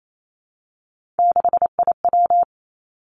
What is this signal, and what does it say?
Morse code, heard as a keyed tone, about 28 wpm, 700 Hz. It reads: 6IW
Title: Three Challenging Letter-Number Combinations 28wpm